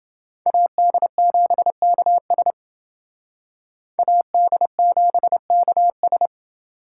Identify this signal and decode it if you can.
Morse code, heard as a keyed tone, about 28 wpm, 700 Hz. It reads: AD7KS AD7KS